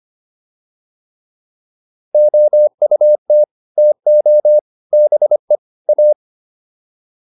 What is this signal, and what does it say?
Morse code, heard as a keyed tone, about 25 wpm, 600 Hz. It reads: OUT TO BE A